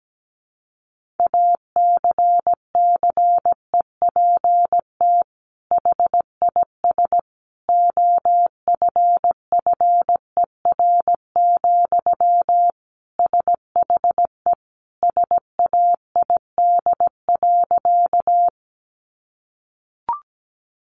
Morse code, heard as a keyed tone, about 17 wpm, 700 Hz. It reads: ACCEPT HIS OFFER, SHE SAID.
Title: Sentences from Top 1300 Words 17wpm